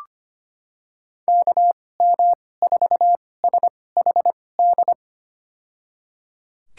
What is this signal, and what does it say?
Morse code, heard as a keyed tone, about 25 wpm, spaced as for 18 wpm, 700 Hz. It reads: KM4SHD
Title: USA Call Signs - 2x Character Spacing - 25wpm